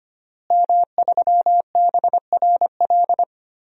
Morse code, heard as a keyed tone, about 25 wpm, 700 Hz. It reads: M3BRL
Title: DXCC-100 Call Signs - 5 Characters - Repeated 3x - 25wpm